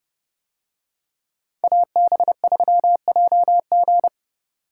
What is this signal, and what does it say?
Morse code, heard as a keyed tone, about 30 wpm, 700 Hz. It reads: AB3JG